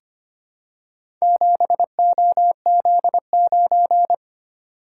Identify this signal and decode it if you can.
Morse code, heard as a keyed tone, about 25 wpm, 700 Hz. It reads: 7OZ9
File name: practice / 25wpm / Four Letter-Number Combinations - 25wpm